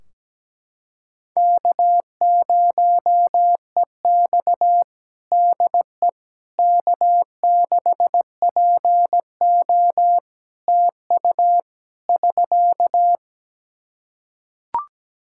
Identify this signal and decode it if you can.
Morse code, heard as a keyed tone, about 17 wpm, 700 Hz. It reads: K0EX DE K6PO TU <SK>
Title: Practice QSO - Exchange 4 - Ending - 17wpm